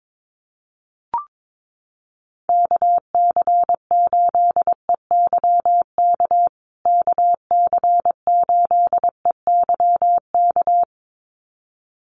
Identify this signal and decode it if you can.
Morse code, heard as a keyed tone, about 22 wpm, 700 Hz. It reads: KC8EYK KC8EYK
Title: USA Call Signs Repeated 2x 22wpm